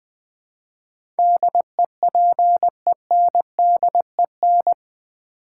Morse code, heard as a keyed tone, about 21 wpm, 700 Hz. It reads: DEPENDEN
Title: Build A Word - 4 Letters At A Time - 10-Letter Words - 20wpm